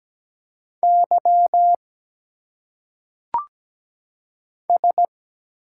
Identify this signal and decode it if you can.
Morse code, heard as a keyed tone, about 17 wpm, 700 Hz. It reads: Y S
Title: Single Letters - Phonetic Pronunciation 17wpm